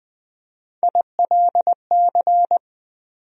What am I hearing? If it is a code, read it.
Morse code, heard as a keyed tone, about 20 wpm, 700 Hz. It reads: ILC
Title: USA Call Sign Postfixes - 3 Letters 20wpm